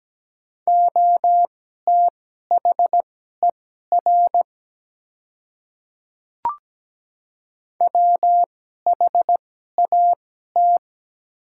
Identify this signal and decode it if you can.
Morse code, heard as a keyed tone, about 17 wpm, spaced as for 12 wpm, 700 Hz. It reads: OTHER WHAT
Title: 2x Character Spacing - Top 100 Words - 17wpm